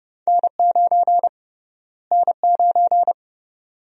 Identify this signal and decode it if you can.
Morse code, heard as a keyed tone, about 30 wpm, 700 Hz. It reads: N9 N9
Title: USA Call Sign Prefixes - Repeated 2x - 30wpm